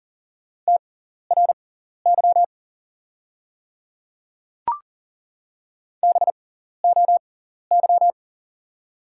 Morse code, heard as a keyed tone, about 37 wpm, spaced as for 14 wpm, 690 Hz. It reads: TRY BOY
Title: 6x Character Spacing - Top 200 Words - 40wpm